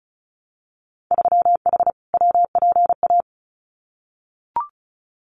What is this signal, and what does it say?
Morse code, heard as a keyed tone, about 36 wpm, 700 Hz. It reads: 3H WPA
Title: Winter Field Day Exchanges Encore 35wpm